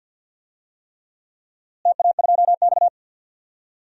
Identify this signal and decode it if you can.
Morse code, heard as a keyed tone, about 50 wpm, 700 Hz. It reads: TA2X
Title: DXCC-100 Call Signs - 4 Characters - Repeated 3x - 50wpm